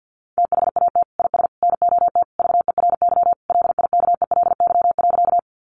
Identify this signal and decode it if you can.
Morse code, heard as a keyed tone, about 50 wpm, 700 Hz. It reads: THAT IS NOT VERY LIKELY.
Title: Sentences from Top 600 Words 50wpm